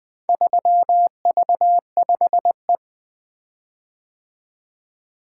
Morse code, heard as a keyed tone, about 20 wpm, 700 Hz. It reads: 3V5E
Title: Four Letter-Number Combinations - 20wpm